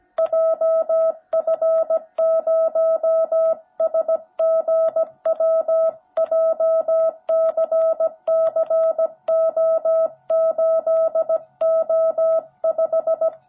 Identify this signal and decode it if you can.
Morse code, heard as a keyed tone, about 17 wpm, 650 Hz. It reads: JF0SGW JCCO8O5
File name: jh8cbh / 2021cw / cw17b